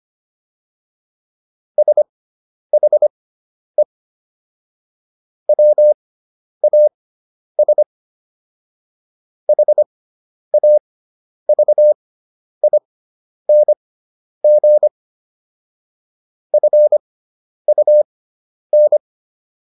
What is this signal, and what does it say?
Morse code, heard as a keyed tone, about 25 wpm, spaced as for 10 wpm, 600 Hz. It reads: SHE WAS HAVING FUN